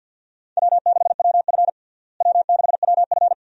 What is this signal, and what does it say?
Morse code, heard as a keyed tone, about 52 wpm, 700 Hz. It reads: W6WF W6WF
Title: USA Call Signs - 4 Characters - Repeated 2x - 50wpm